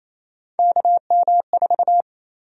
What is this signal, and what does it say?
Morse code, heard as a keyed tone, about 28 wpm, 700 Hz. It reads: KM4
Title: USA Call Sign Prefixes - 28wpm